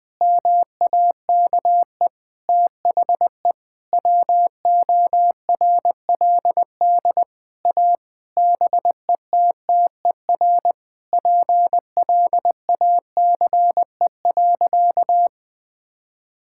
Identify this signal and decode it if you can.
Morse code, heard as a keyed tone, about 21 wpm, 700 Hz. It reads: MAKE THE WORLD A BETTER PLACE.
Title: Sentences with 7-Word Phrases - 20wpm